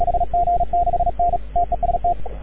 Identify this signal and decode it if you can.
Morse code, heard as a keyed tone, about 36 wpm, 700 Hz, with radio noise. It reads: HG6N TEST
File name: AUDIO2019 / HG6N CW SSB MIX